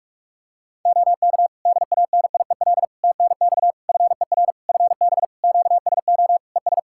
Morse code, heard as a keyed tone, about 46 wpm, 700 Hz. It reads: OK DANIEL TNX FER FB QSO ES
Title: Practice QSO - Exchange 4 - Ending - 45wpm